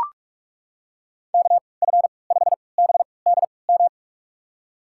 Morse code, heard as a keyed tone, about 43 wpm, spaced as for 25 wpm, 700 Hz. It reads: KF5BDM
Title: USA Call Signs - 6 Characters - 3x Character Spacing - 45wpm